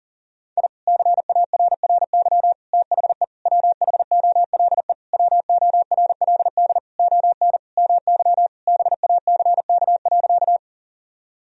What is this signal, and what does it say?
Morse code, heard as a keyed tone, about 40 wpm, 700 Hz. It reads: I CARRY THE WHOLE WORLD ON MY BACK.
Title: Sentences from Top 1000 Words - Review Part 1 - 40wpm